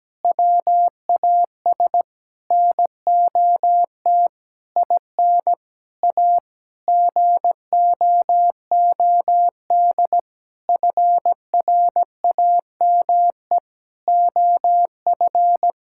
Morse code, heard as a keyed tone, about 17 wpm, 700 Hz. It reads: WAS NOT IN A GOOD FRAME OF
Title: Sentences from Top 1000 Words - Review Part 6 - 17wpm